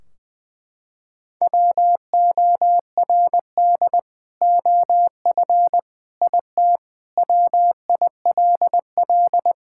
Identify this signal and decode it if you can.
Morse code, heard as a keyed tone, about 21 wpm, 700 Hz. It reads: WORD OF IT WILL
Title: Sentences from Top 1600 Words 20wpm